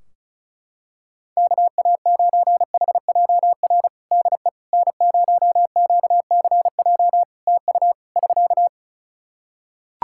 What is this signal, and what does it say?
Morse code, heard as a keyed tone, about 36 wpm, 700 Hz. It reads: KA9HJR DE N0QCJ TU <SK>
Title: Practice QSO - Exchange 4 - Ending - 35wpm